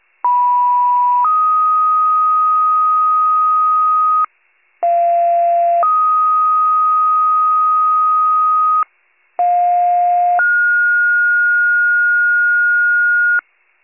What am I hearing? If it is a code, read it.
Two-tone page, heard as a one-second tone, then a three-second tone, about 980 then 1250 Hz, about 690 then 1250 Hz, about 710 then 1430 Hz. It is heard from 0.2 to 13.4 s.